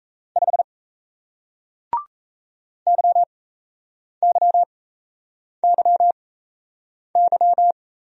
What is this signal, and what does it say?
Morse code, heard as a keyed tone, about 38 wpm, spaced as for 8 wpm, 700 Hz. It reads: 5 YYYY